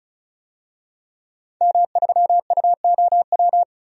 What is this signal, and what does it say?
Morse code, heard as a keyed tone, about 36 wpm, 700 Hz. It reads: M3UOW
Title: DXCC-100 Call Signs - 5 Characters - 35wpm